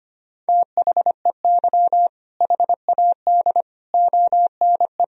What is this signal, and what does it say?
Morse code, heard as a keyed tone, about 25 wpm, 700 Hz. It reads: THEY HAD ONE
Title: Sentences from Top 1000 Words - Review Part 3 - 25wpm